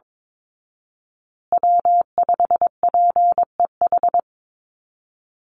Morse code, heard as a keyed tone, about 22 wpm, 690 Hz. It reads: W5PEH
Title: Call Signs Repeated 3x - 22wpm